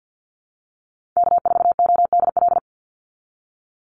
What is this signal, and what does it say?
Morse code, heard as a keyed tone, about 48 wpm, 700 Hz. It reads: K4ODL
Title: Call Signs Repeated 2x - 50wpm